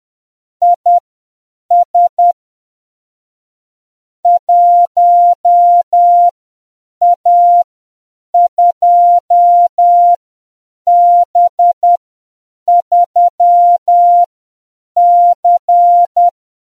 Morse code, heard as a keyed tone, about 10 wpm, spaced as for 7 wpm, 700 Hz. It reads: IS 1A2B3C